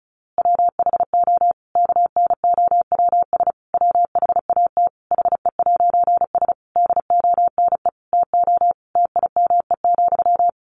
Morse code, heard as a keyed tone, about 34 wpm, 700 Hz. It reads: WHO KNOWS WHAT HE'S DONE TO TIME,
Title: A Sound Of Thunder 35wpm